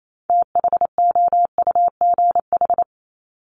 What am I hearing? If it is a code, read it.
Morse code, heard as a keyed tone, about 28 wpm, 700 Hz. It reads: THOUGH
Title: Top 500 Words 28wpm